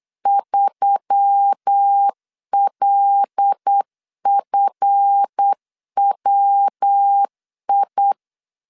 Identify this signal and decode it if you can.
Morse code, heard as a keyed tone, about 8 wpm, 800 Hz. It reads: 3LFWI
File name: LEZIONI / 8 YJ45 / 8 YJ4 8wpm 3246